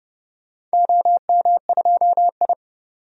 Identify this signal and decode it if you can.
Morse code, heard as a keyed tone, about 30 wpm, 700 Hz. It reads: OM2I
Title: DXCC-100 Call Signs - 4 Characters - 30wpm